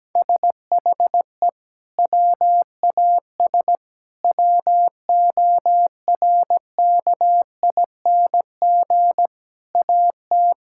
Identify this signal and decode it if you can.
Morse code, heard as a keyed tone, about 17 wpm, 700 Hz. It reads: SHE WAS WORKING AT
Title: Sentences from Top 1000 Words - Review Part 5 - 17wpm